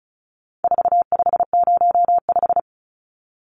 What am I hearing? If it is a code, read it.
Morse code, heard as a keyed tone, about 34 wpm, 700 Hz. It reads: 4505
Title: Numbers - 4-Digit 35wpm